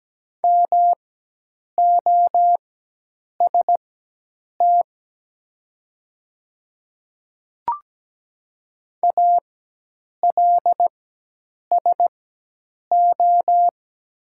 Morse code, heard as a keyed tone, about 17 wpm, spaced as for 8 wpm, 700 Hz. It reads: MOST ALSO
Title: 4x Character Spacing - Top 200 Words - 17wpm